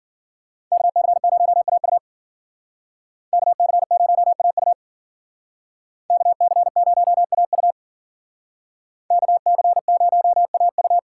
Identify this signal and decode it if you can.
Morse code, heard as a keyed tone, about 49 wpm, 700 Hz. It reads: KC0AU KC0AU KC0AU KC0AU
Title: Speed Racing - USA 2x2 Call Signs 40wpm